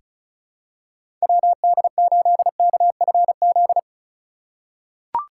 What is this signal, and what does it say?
Morse code, heard as a keyed tone, about 34 wpm, 700 Hz. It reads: WD8KFZ
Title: USA Call Signs - 6 Characters - 3x Character Spacing - 35wpm